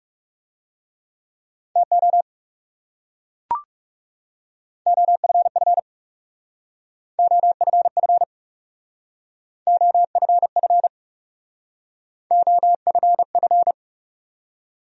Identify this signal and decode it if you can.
Morse code, heard as a keyed tone, about 36 wpm, 700 Hz. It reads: TO OFF OFF OFF OFF
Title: Speed-Racing Top 200 Words 30wpm